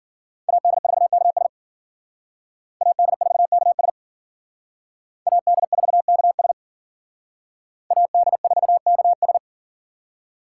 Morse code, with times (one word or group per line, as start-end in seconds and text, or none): 0.5-1.5 AD4KS
2.8-3.9 AD4KS
5.3-6.5 AD4KS
7.9-9.4 AD4KS